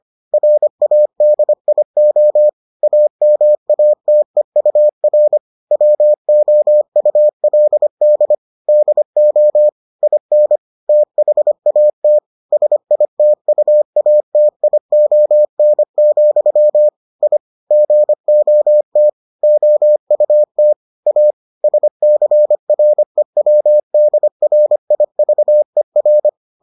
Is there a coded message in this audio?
Yes: Morse code from 0.3 to 26.3 s, heard as a keyed tone, about 25 wpm, 600 Hz. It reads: RADIO AMATEUR WOULD DO IN THAT SITUATION, I GOT OUT A SCREWDRIVER